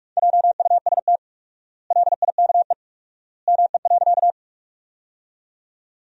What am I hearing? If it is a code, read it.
Morse code, heard as a keyed tone, about 47 wpm, 700 Hz. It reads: JUST LIKE ME.
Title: 4x Word-Spacing - Sentences from Top 200 Words - 45wpm